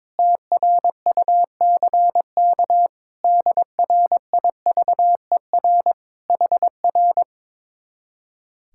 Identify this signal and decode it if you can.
Morse code, heard as a keyed tone, about 22 wpm, 700 Hz. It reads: TRUCK DRIVER HR
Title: Speed-Racing Common QSO Phrases - 22wpm